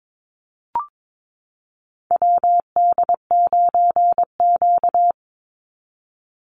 Morse code, heard as a keyed tone, about 22 wpm, 700 Hz. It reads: WD9Q
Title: Call Signs Repeated 3x - 22wpm